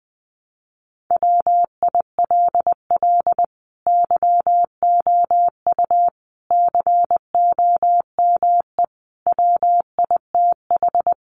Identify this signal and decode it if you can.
Morse code, heard as a keyed tone, about 20 wpm, 700 Hz. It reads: WILL YOU COME WITH
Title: Sentences from Top 200 Words 20wpm